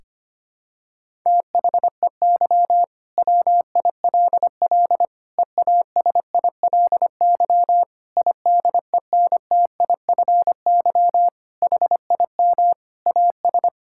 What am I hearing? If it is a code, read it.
Morse code, heard as a keyed tone, about 25 wpm, 700 Hz. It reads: THEY WILL EASILY IDENTIFY HIM AS
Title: Sentences from Top 1000 Words - Review Part 6 - 25wpm